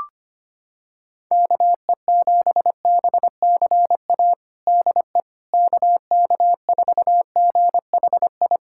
Morse code, heard as a keyed tone, about 25 wpm, 700 Hz. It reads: KE7BCA DE KK4GHI
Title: USA Callsign DE Callsign 25wpm